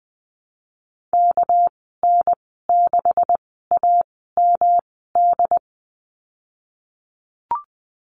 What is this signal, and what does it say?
Morse code, heard as a keyed tone, about 20 wpm, spaced as for 15 wpm, 700 Hz. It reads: KN6AMD
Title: 207 Lesson 33g - Letter j - Call Signs 20wpm